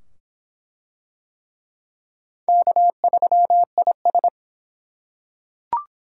Morse code, heard as a keyed tone, about 22 wpm, 700 Hz. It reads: K3IS